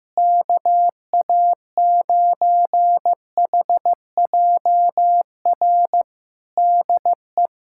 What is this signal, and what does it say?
Morse code, heard as a keyed tone, about 15 wpm, 700 Hz. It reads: KA9HJR DE
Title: Practice QSO - Exchange 4 - Ending - 15wpm